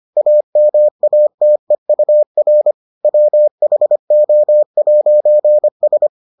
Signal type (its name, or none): Morse code